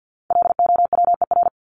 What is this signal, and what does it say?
Morse code, heard as a keyed tone, about 50 wpm, 700 Hz. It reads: LOWER
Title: Top 501-750 Words 50wpm